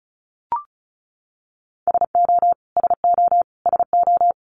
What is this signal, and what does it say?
Morse code, heard as a keyed tone, about 34 wpm, 700 Hz. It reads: SO SO SO
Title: Top 500 Words Repeated 35wpm